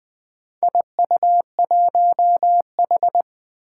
Morse code, heard as a keyed tone, about 20 wpm, 700 Hz. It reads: IU1H
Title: Four Challenging Letter-Number Combinations 20wpm